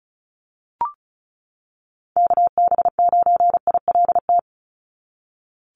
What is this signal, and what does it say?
Morse code, heard as a keyed tone, about 36 wpm, 700 Hz. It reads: KB9ILT
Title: NA SOTA Activator Callsigns 35wpm